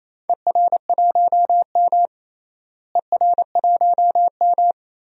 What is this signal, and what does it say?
Morse code, heard as a keyed tone, about 28 wpm, 700 Hz. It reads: ER1M ER1M
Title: DXCC-100 Call Signs - 4 Characters - Repeated 3x - 28wpm